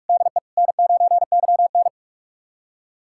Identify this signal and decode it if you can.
Morse code, heard as a keyed tone, about 41 wpm, 700 Hz. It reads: DE N9YN
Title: USA Callsign DE Callsign - 4 Characters - 45wpm